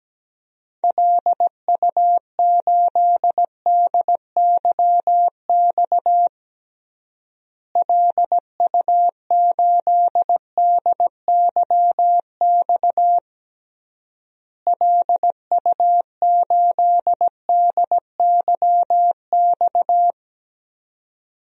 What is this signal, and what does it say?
Morse code, heard as a keyed tone, about 17 wpm, 700 Hz. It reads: LU8DYX LU8DYX LU8DYX
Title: DXCC-100 Call Signs - 6 Characters - Repeated 3x - 17wpm